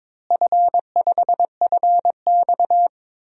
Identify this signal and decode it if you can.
Morse code, heard as a keyed tone, about 22 wpm, 700 Hz. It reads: F5FX
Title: DXCC-100 Call Signs - 4 Characters - Repeated 3x - 22wpm